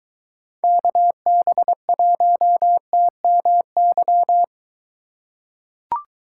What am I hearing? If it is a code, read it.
Morse code, heard as a keyed tone, about 23 wpm, 700 Hz. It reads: KB1TMY